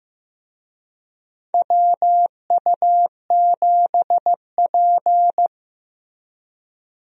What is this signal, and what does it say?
Morse code, heard as a keyed tone, about 15 wpm, 700 Hz. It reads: WU7P
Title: USA Call Signs - 4 Characters - Repeated 2x - 15wpm